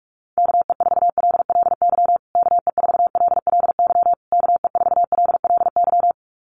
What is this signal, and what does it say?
Morse code, heard as a keyed tone, about 44 wpm, 700 Hz. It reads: KE4LLY KE4LLY KE4LLY
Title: USA Call Signs Repeated 3x 45wpm